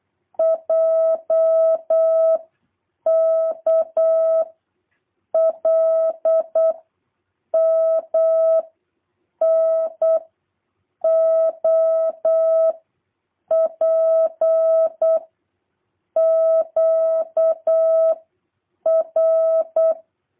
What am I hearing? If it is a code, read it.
Morse code, heard as a keyed tone, about 8 wpm, 650 Hz. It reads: J K L M N O P Q R
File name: jh8cbh / 2023cw